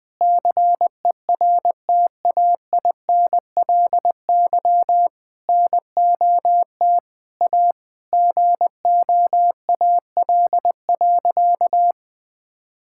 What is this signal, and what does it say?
Morse code, heard as a keyed tone, about 20 wpm, 700 Hz. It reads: CERTAINLY NOT A GOAL.
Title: Sentences from Top 1000 Words - Review Part 5 - 20wpm